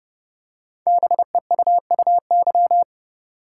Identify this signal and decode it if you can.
Morse code, heard as a keyed tone, about 30 wpm, 700 Hz. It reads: BEUUY
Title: Code Groups - 5 Letters - 30wpm